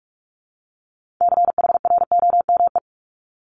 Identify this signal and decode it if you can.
Morse code, heard as a keyed tone, about 46 wpm, 700 Hz. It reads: CHROME
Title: Multi-letter Phonemes Repeated 3x 45wpm